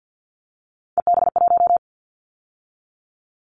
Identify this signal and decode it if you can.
Morse code, heard as a keyed tone, about 50 wpm, 700 Hz. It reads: EB1
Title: Callsign Prefixes for Top 25 DXCC - 50wpm